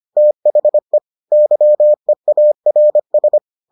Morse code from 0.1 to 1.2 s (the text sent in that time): THE